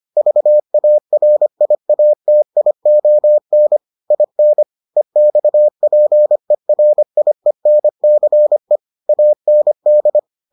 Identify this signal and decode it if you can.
Morse code, heard as a keyed tone, about 25 wpm, 600 Hz. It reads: VARIATION IN EXPERIENCE AND